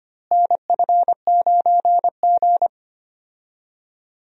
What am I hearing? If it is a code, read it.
Morse code, heard as a keyed tone, about 25 wpm, 700 Hz. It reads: NF9G
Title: USA Call Signs - 4 Characters - Repeated 3x - 25wpm